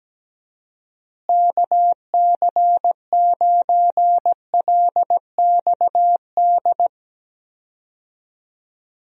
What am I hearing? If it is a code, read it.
Morse code, heard as a keyed tone, about 17 wpm, 700 Hz. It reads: KC9LXD
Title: USA Call Signs - 6 Characters - Repeated 3x - 17wpm